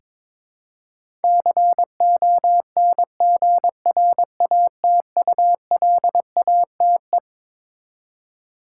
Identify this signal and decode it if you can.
Morse code, heard as a keyed tone, about 22 wpm, 700 Hz. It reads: CONGRATULATE
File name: Twelve Letter Words - 22wpm